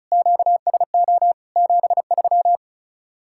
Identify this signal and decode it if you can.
Morse code, heard as a keyed tone, about 44 wpm, 700 Hz. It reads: QSO 73